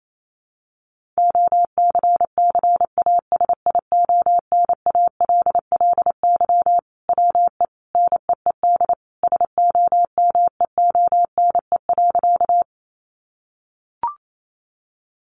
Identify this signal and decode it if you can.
Morse code, heard as a keyed tone, about 28 wpm, 700 Hz. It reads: OCCASIONALLY WE NEED SOMEONE.
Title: Sentences from Top 1300 Words 28wpm